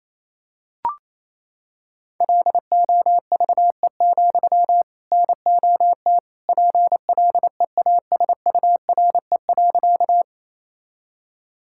Morse code, heard as a keyed tone, about 28 wpm, 700 Hz. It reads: LOVE, NOT PLEASURE.
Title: Sentences from Top 1600 Words 28wpm